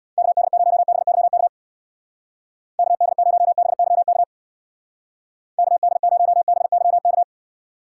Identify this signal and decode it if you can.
Morse code, heard as a keyed tone, about 65 wpm, 700 Hz. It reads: KD0BYK KD0BYK KD0BYK